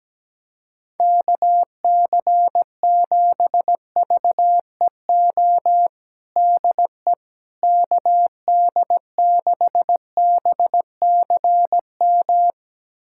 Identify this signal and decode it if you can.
Morse code, heard as a keyed tone, about 17 wpm, 700 Hz. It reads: KC7VEO DE KD6BCM
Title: USA Callsign DE Callsign - 6 Characters - 17wpm